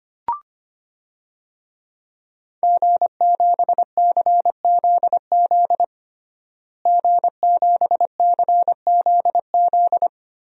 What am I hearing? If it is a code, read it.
Morse code, heard as a keyed tone, about 25 wpm, 700 Hz. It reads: G7CZZ G7CZZ
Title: DXCC-100 Call Signs - 5 Characters - Repeated 2x - 25wpm